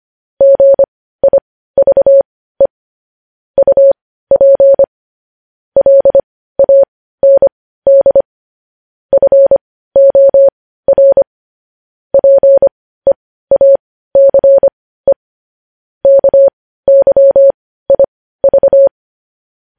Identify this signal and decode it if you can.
Morse code, heard as a keyed tone, about 25 wpm, spaced as for 15 wpm, 550 Hz. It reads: GIVE UP LAND FOR PEACE KYIV
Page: news-headlines-in-morse-code-at-15-wpm